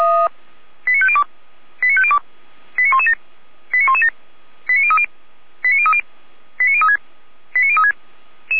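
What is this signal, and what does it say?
Five-tone selective call (ZVEI), heard as quick runs of five tones, tones of about 70 ms. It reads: 89502 89502 89117 89117 89030 89030 89036 89036
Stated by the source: ZVEI five-tone selective call 89115 received by SDR.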